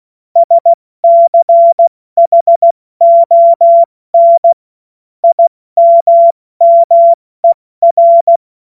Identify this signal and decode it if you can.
Morse code, heard as a keyed tone, about 16 wpm, 680 Hz. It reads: SCHON IMMER
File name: Morsekurs / Witze / Witz 09. Dezember 2025 70 F80